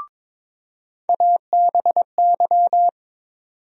Morse code, heard as a keyed tone, about 21 wpm, 700 Hz. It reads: ABY
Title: USA Call Sign Postfixes - 3 Letters 22wpm